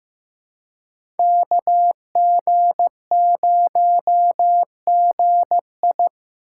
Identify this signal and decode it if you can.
Morse code, heard as a keyed tone, about 15 wpm, 700 Hz. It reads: KG0GI
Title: USA Call Signs - 5 Characters - 2x Character Spacing - 15wpm